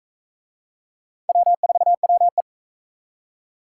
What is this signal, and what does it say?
Morse code, heard as a keyed tone, about 44 wpm, 700 Hz. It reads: W4WE